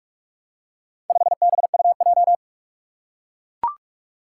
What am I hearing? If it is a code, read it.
Morse code, heard as a keyed tone, about 41 wpm, 700 Hz. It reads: 5BUJ
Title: Four Challenging Letter-Number Combinations 45wpm